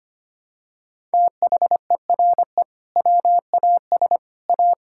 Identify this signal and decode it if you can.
Morse code, heard as a keyed tone, about 25 wpm, 700 Hz. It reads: THERE WAS A
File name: practice / 25wpm / Sentences with 2-Word Phrases - 25wpm